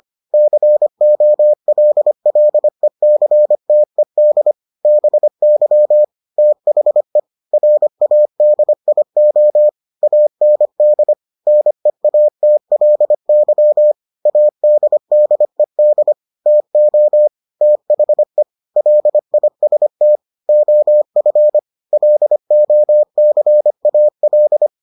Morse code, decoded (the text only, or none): COLLECTED BY THE RADIO AND NEATLY ADDED TO THE LIST OF LOCAL